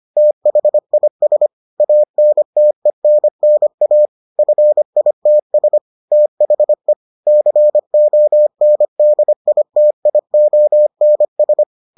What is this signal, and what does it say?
Morse code, heard as a keyed tone, about 25 wpm, 600 Hz. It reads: THIS ANTENNA FITS THE CONDITIONS